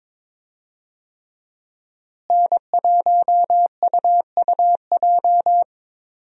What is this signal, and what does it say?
Morse code, heard as a keyed tone, about 21 wpm, 700 Hz. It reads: N1UUJ
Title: USA Call Signs - 5 Characters - Repeated 3x - 22wpm